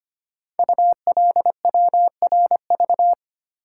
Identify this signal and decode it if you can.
Morse code, heard as a keyed tone, about 25 wpm, 700 Hz. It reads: ULWRV
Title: Code Groups - 5 Letters - 25wpm